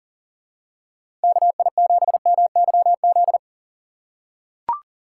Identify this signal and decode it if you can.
Morse code, heard as a keyed tone, about 40 wpm, 700 Hz. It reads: KI7MYZ
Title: USA Call Signs - 6 Characters - 6x Character Spacing - 40wpm